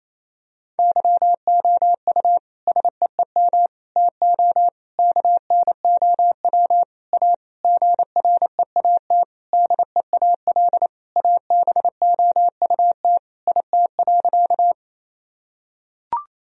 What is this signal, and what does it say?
Morse code, heard as a keyed tone, about 28 wpm, 700 Hz. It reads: YOU SEEM TO KNOW A GREAT DEAL ABOUT IT.
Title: Sentences from Top 1000 Words - Review Part 1 - 28wpm